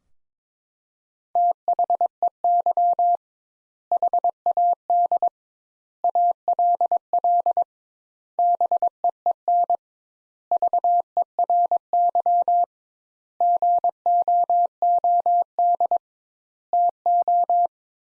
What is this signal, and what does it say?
Morse code, heard as a keyed tone, about 22 wpm, 700 Hz. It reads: THEY HAD ALL BEEN VERY GOOD TO
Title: Sentences Top 200 Words - 22wpm